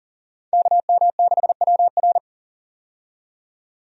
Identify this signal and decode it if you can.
Morse code, heard as a keyed tone, about 39 wpm, 700 Hz. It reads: KM6WR
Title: USA Call Signs - 5 Characters - 40wpm